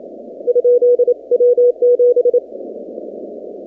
Morse code, heard as a keyed tone, about 27 wpm, 490 Hz. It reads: ? W7
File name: 2019-cw-wpx-kt5j-k5pi